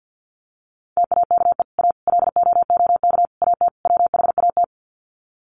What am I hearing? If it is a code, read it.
Morse code, heard as a keyed tone, about 48 wpm, 700 Hz. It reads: TAKE A LOOK AT WHAT
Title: Sets of 5 Words Encore2 50wpm